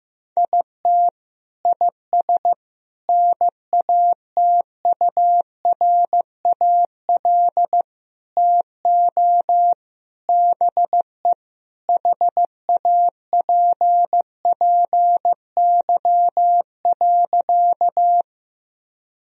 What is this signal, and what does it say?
Morse code, heard as a keyed tone, about 15 wpm, 700 Hz. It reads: IT IS NATURAL TO BE HAPPY.
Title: Sentences from Top 1000 Words - Review Part 4 - 15wpm